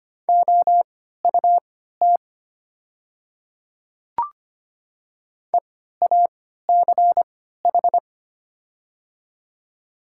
Morse code, heard as a keyed tone, about 25 wpm, spaced as for 14 wpm, 700 Hz. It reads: OUT EACH
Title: 3x Character Spacing - Top 300 Words - 25wpm